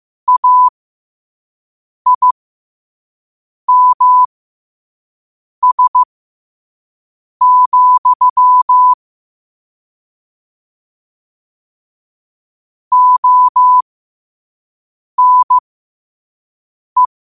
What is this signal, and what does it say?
Morse code, heard as a keyed tone, about 15 wpm, spaced as for 5 wpm, 1000 Hz. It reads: AIMS, ONE
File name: qotdfiles / 5wpm